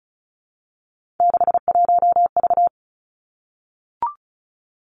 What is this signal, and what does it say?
Morse code, heard as a keyed tone, about 36 wpm, 700 Hz. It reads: 61V
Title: Three Challenging Letter-Number Combinations 35wpm